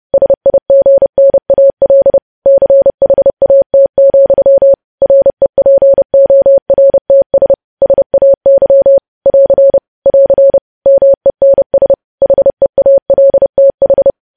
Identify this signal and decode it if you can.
Morse code, heard as a keyed tone, about 30 wpm, 550 Hz. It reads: SIGNAL CHAT, REPORTS SAY <AR> <AR> MENS HEALTH
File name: podcast / News headlines-1x-30x30